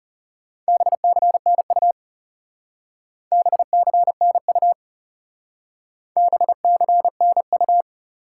Morse code, heard as a keyed tone, about 37 wpm, 700 Hz. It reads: BCNU BCNU BCNU
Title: Speed-Racing Top 90 QSO Elements 30wpm